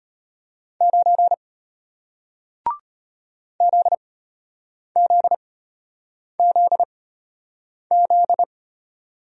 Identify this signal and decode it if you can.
Morse code, heard as a keyed tone, about 33 wpm, spaced as for 8 wpm, 700 Hz. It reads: 9 ZZZZ